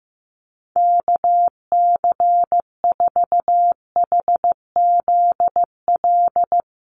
Morse code, heard as a keyed tone, about 15 wpm, 700 Hz. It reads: KC4HZL
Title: USA Call Signs Repeated 3x 15wpm